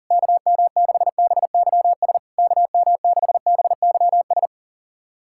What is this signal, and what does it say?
Morse code, heard as a keyed tone, about 42 wpm, 690 Hz. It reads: KM6BYS KM6BYS
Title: SOTA QSO Practice 40wpm